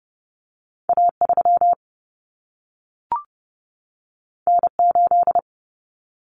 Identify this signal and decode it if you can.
Morse code, heard as a keyed tone, about 30 wpm, 700 Hz. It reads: A3 N8
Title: Common Cut Numbers - 2-Digit 30wpm